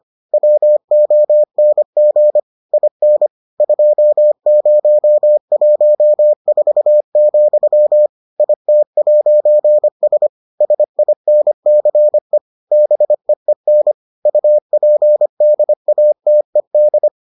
Morse code, heard as a keyed tone, about 25 wpm, 600 Hz. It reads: WONG IN 2014, IT'S SINCE BEEN UPDATED